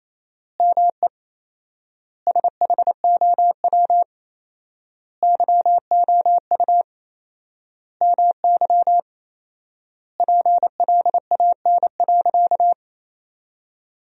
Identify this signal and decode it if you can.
Morse code, heard as a keyed tone, about 28 wpm, 700 Hz. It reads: ME SHOW YOU MY PLAN.
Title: 4x Word-Spacing - Sentences from Top 500 Words - 28wpm